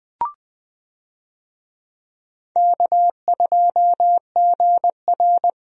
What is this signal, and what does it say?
Morse code, heard as a keyed tone, about 20 wpm, 700 Hz. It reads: K2GR
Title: USA Call Signs - 4 Characters - Repeated 2x - 20wpm